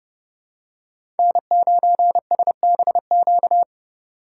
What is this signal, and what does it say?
Morse code, heard as a keyed tone, about 32 wpm, 700 Hz. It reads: N9SBQ
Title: USA Call Signs - 5 Characters - 2x Character Spacing - 30wpm